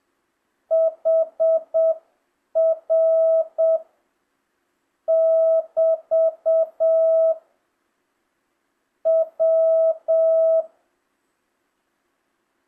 Morse code, heard as a keyed tone, about 7 wpm, 650 Hz. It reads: HR <BT> W
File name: jh8cbh / 2025cw